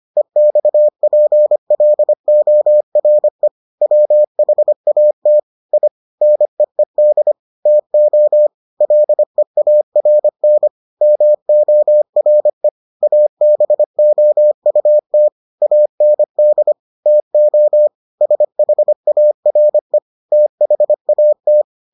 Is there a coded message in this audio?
Yes: Morse code from 0.2 to 21.6 s, heard as a keyed tone, about 25 wpm, 600 Hz. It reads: EXPLORE WHAT I NEED TO LEARN MORE ABOUT AND TO SHARE THAT